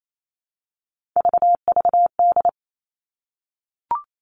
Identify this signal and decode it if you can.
Morse code, heard as a keyed tone, about 28 wpm, 700 Hz. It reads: VVD
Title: Three Challenging Letter-Number Combinations 28wpm